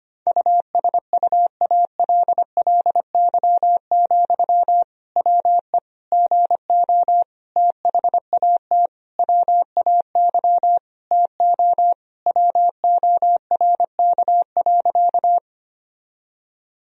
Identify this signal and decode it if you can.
Morse code, heard as a keyed tone, about 25 wpm, 700 Hz. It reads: USUALLY, WE GO THAT WAY TO WORK.
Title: Sentences from Top 1000 Words - Review Part 3 - 25wpm